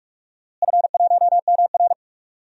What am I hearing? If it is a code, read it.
Morse code, heard as a keyed tone, about 46 wpm, 700 Hz. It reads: F1MR
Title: DXCC-100 Call Signs - 4 Characters - Repeated 2x - 45wpm